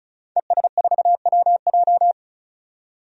Morse code, heard as a keyed tone, about 34 wpm, 700 Hz. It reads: ES4WJ
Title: DXCC-100 Call Signs - 5 Characters - 35wpm